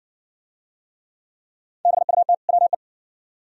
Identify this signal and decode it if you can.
Morse code, heard as a keyed tone, about 63 wpm, 700 Hz. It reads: BUT WE